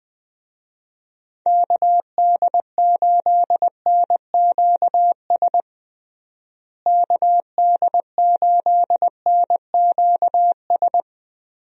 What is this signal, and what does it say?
Morse code, heard as a keyed tone, about 20 wpm, 700 Hz. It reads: KD8NQS KD8NQS
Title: USA Call Signs - 6 Characters - Repeated 3x - 20wpm